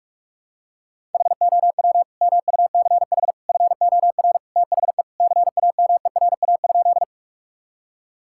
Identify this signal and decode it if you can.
Morse code, heard as a keyed tone, about 46 wpm, 700 Hz. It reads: HOW MUCH FOR THE CAMERA?
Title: Sentences from Top 1000 Words - Review Part 5 - 45wpm